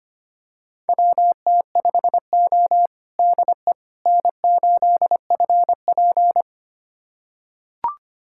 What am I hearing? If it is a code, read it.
Morse code, heard as a keyed tone, about 25 wpm, 700 Hz. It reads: WT5O DE N8FP
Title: USA Callsign DE Callsign - 4 Characters - 25wpm